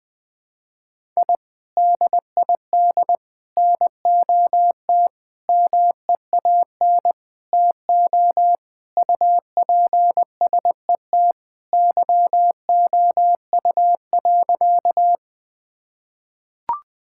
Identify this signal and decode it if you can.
Morse code, heard as a keyed tone, about 21 wpm, 700 Hz. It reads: I DID NOT MEAN TO UPSET YOU.
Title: Sentences with 5-Word Phrases - 20wpm